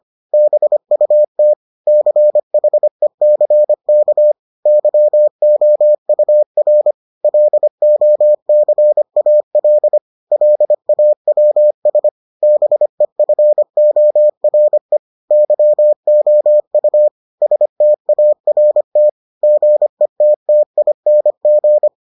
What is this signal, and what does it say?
Morse code, heard as a keyed tone, about 25 wpm, 600 Hz. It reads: BUT CHECK YOUR LOCAL LAWS BEFORE YOU START GETTING